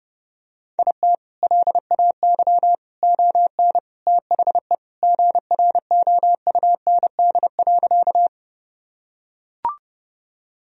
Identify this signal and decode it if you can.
Morse code, heard as a keyed tone, about 30 wpm, 700 Hz. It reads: IT LAY ON THE GROUND.
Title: Sentences from Top 1000 Words - Review Part 2 - 30wpm